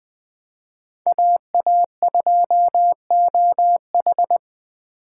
Morse code, heard as a keyed tone, about 20 wpm, 700 Hz. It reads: AA2OH
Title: Speed Racing - USA 2x2 Call Signs 17wpm